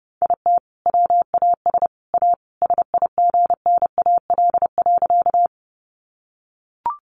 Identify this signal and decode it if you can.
Morse code, heard as a keyed tone, about 30 wpm, 700 Hz. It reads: IT WAS A SIGNAL.
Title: Sentences from Top 1400 Words 30wpm